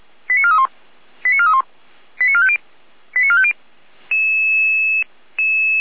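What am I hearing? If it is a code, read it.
Five-tone selective call (ZVEI), heard as quick runs of five tones, tones of about 70 ms. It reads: 89431 89431 89450 89450
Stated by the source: ZVEI five-tone selective call received on SDR, 455 channel path.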